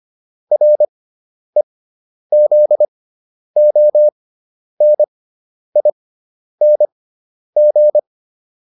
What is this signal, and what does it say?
Morse code, heard as a keyed tone, about 25 wpm, spaced as for 10 wpm, 600 Hz. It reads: REZONING